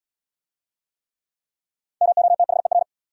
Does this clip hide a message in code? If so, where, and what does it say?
Yes: Morse code from 2.0 to 2.8 s, heard as a keyed tone, about 64 wpm, 700 Hz. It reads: MOTHER